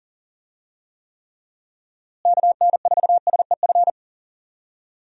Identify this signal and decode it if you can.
Morse code, heard as a keyed tone, about 40 wpm, 700 Hz. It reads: KN4SEF
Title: USA Call Signs - 6 Characters - Repeated 2x - 40wpm